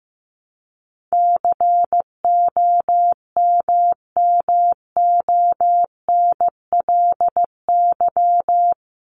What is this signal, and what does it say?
Morse code, heard as a keyed tone, about 15 wpm, 700 Hz. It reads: COMMONLY
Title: LY Suffix - Top 100 Words with Base Words 15wpm